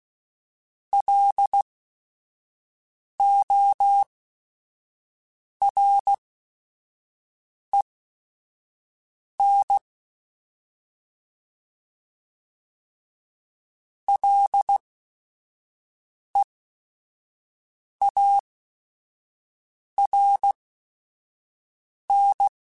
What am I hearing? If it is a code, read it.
Morse code, heard as a keyed tone, about 16 wpm, spaced as for 5 wpm, 800 Hz. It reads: LOREN LEARN